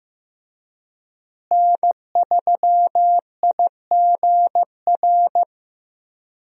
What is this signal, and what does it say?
Morse code, heard as a keyed tone, about 15 wpm, 700 Hz. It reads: N3IGR
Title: USA Call Signs - 5 Characters - Repeated 3x - 15wpm